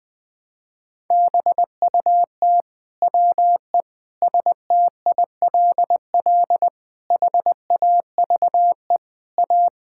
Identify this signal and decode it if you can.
Morse code, heard as a keyed tone, about 21 wpm, 700 Hz. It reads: BUT WE STILL HAVE A
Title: Sets of 9 Words - 20wpm